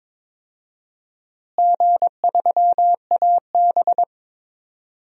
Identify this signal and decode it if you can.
Morse code, heard as a keyed tone, about 21 wpm, 700 Hz. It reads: G3AB
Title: DXCC-100 Call Signs - 4 Characters - Repeated 3x - 22wpm